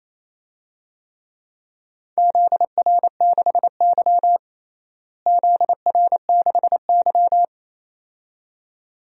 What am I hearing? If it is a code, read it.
Morse code, heard as a keyed tone, about 28 wpm, 700 Hz. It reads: ZR6Y ZR6Y
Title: DXCC-100 Call Signs - 4 Characters - Repeated 2x - 28wpm